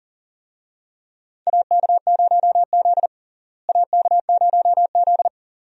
Morse code, heard as a keyed tone, about 40 wpm, 700 Hz. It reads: AK0Z AK0Z
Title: USA Call Signs - 4 Characters - Repeated 2x - 40wpm